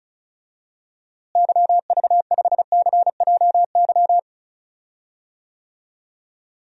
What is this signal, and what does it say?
Morse code, heard as a keyed tone, about 34 wpm, 700 Hz. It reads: YV5CJY
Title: DXCC-100 Call Signs - 6 Characters - Repeated 2x - 35wpm